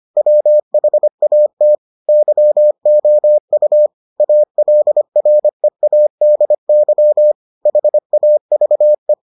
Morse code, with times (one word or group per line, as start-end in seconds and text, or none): 0.2-1.8 WHAT
2.1-3.9 YOU
4.2-7.3 ALREADY
7.7-9.1 HAVE